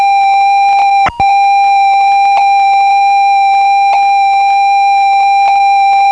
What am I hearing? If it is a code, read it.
Computer data on cassette.